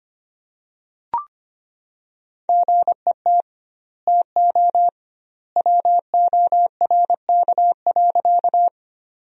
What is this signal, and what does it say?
Morse code, heard as a keyed tone, about 25 wpm, 700 Hz. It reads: GET TO WORK.
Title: 2x Word-Spacing - Sentences from Top 200 Words - 25wpm